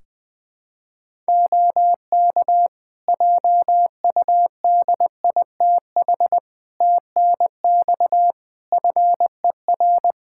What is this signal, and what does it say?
Morse code, heard as a keyed tone, about 20 wpm, 700 Hz. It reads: OK JUDITH TNX FER
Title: Practice QSO - Exchange 4 - Ending - 20wpm